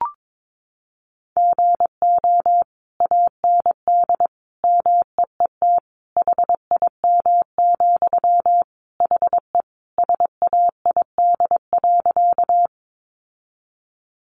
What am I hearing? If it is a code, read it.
Morse code, heard as a keyed tone, about 22 wpm, 700 Hz. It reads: GO AND MEET HIM, HE SAID.
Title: Sentences from Top 1100 Words 22wpm